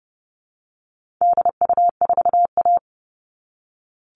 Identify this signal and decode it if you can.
Morse code, heard as a keyed tone, about 30 wpm, 700 Hz. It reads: DU4A
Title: All Cut Numbers - 4-Digit 30wpm